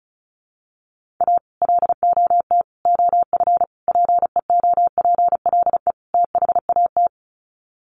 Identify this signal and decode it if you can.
Morse code, heard as a keyed tone, about 34 wpm, 700 Hz. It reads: A LOT OF PEOPLE THAT
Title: Sets of 5 Words Encore2 35wpm